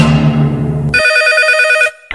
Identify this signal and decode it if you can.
There is no code.